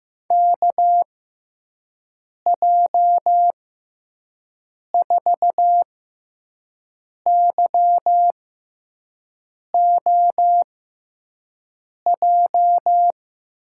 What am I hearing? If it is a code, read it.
Morse code, heard as a keyed tone, about 15 wpm, spaced as for 5 wpm, 700 Hz. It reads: KJ4YOJ
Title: USA Call Signs - 6 Characters - 6x Character Spacing - 15wpm